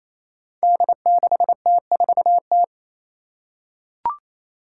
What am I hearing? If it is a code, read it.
Morse code, heard as a keyed tone, about 28 wpm, 700 Hz. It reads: D6T4T
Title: Five Challenging Letter-Number Combinations 28wpm